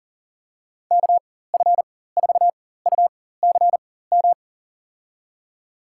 Morse code, heard as a keyed tone, about 37 wpm, spaced as for 18 wpm, 700 Hz. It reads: KF4UCM
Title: USA Call Signs - 6 Characters - 4x Character Spacing - 40wpm